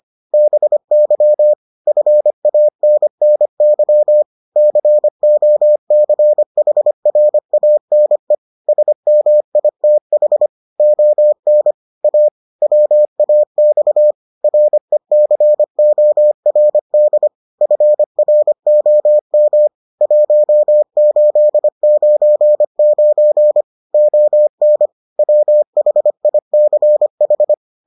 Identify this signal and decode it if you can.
Morse code, heard as a keyed tone, about 25 wpm, 600 Hz. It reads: BY FANNY COCHRANE SMITH ON A WAX RECORD FROM 1899 ON WHICH